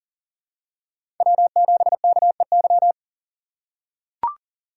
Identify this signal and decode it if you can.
Morse code, heard as a keyed tone, about 40 wpm, 700 Hz. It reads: W7KEY
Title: USA Call Signs - 5 Characters - 4x Character Spacing - 40wpm